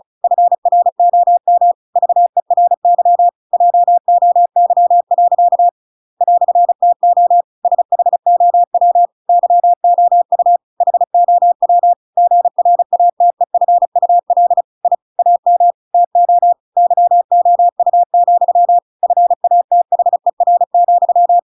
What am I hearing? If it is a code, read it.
Morse code, heard as a keyed tone, about 34 wpm, 700 Hz. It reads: FROM VERY JOY. "TO SHOW YOU HOW GRATEFUL I AM TO YOU, FATHER,
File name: Pinocchio / Pinocchio ch8 35 35 ik1hge